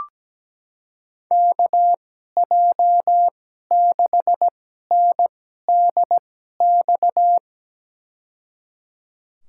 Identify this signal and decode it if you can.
Morse code, heard as a keyed tone, about 17 wpm, spaced as for 12 wpm, 700 Hz. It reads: KJ6NDX
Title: USA Call Signs - 6 Characters - 2x Character Spacing - 17wpm